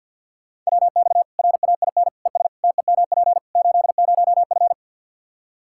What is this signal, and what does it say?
Morse code, heard as a keyed tone, about 46 wpm, 700 Hz. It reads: WX RAIN ES TEMP 80F
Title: Practice QSO - Exchange 2 - Info - 50wpm